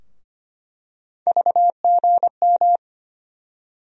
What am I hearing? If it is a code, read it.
Morse code, heard as a keyed tone, about 25 wpm, 700 Hz. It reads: VGM
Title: USA Call Sign Postfixes - 3 Letters 25wpm